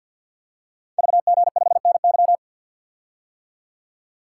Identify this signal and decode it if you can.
Morse code, heard as a keyed tone, about 52 wpm, 700 Hz. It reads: VG5NY
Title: Code Groups - 5 Letter:Numbers - 50wpm